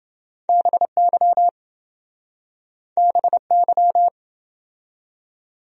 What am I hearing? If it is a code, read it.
Morse code, heard as a keyed tone, about 29 wpm, 700 Hz. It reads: BY BY